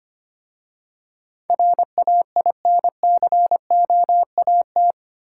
Morse code, heard as a keyed tone, about 25 wpm, 690 Hz. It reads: RAINCOAT
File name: practice / 25wpm / Easy Words with Hints - 25wpm